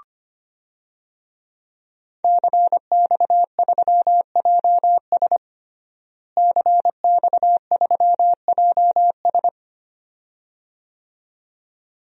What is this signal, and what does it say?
Morse code, heard as a keyed tone, about 25 wpm, 700 Hz. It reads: CX3JS CX3JS
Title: DXCC-100 Call Signs - 5 Characters - Repeated 2x - 25wpm